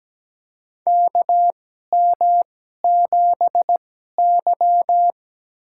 Morse code, heard as a keyed tone, about 17 wpm, spaced as for 12 wpm, 700 Hz. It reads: KM7Y
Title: USA Call Signs - 4 Characters - 2x Character Spacing - 17wpm